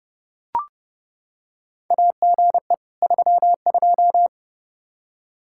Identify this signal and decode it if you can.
Morse code, heard as a keyed tone, about 30 wpm, 700 Hz. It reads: AGE 32
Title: K1USN SST QSO Practice 30wpm